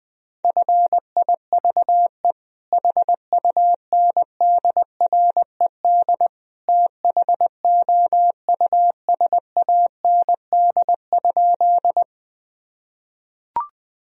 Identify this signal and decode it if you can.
Morse code, heard as a keyed tone, about 20 wpm, 700 Hz. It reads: FIVE HUNDRED THOUSAND?
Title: Sentences from Top 1000 Words - Review Part 3 - 20wpm